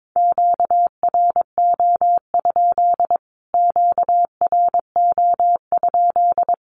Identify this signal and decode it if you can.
Morse code, heard as a keyed tone, about 22 wpm, 700 Hz. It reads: QRO? QRO?
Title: Q-Codes Repeated 3x 22wpm